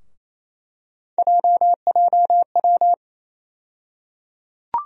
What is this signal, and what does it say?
Morse code, heard as a keyed tone, about 28 wpm, 700 Hz. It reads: JJW
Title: USA Call Sign Postfixes - 3 Letters 28wpm